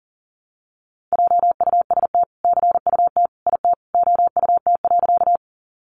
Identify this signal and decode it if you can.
Morse code, heard as a keyed tone, about 40 wpm, 700 Hz. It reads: JUST CUT IT OUT.
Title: Sentences from Top 300 Words 40wpm